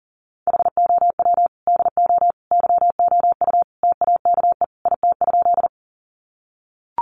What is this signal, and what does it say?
Morse code, heard as a keyed tone, about 40 wpm, 700 Hz. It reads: HOW DO YOU TAKE IT?
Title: Sentences from Top 200 Words 40wpm